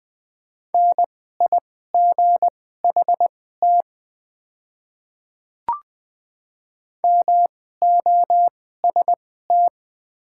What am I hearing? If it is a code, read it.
Morse code, heard as a keyed tone, about 20 wpm, spaced as for 14 wpm, 690 Hz. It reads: NIGHT MOST
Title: 2x Character Spacing - Top 300 Words - 20wpm